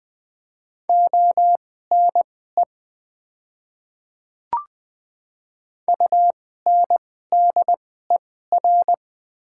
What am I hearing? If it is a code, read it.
Morse code, heard as a keyed tone, about 21 wpm, spaced as for 15 wpm, 700 Hz. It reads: ONE UNDER
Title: 2x Character Spacing - Top 300 Words - 20wpm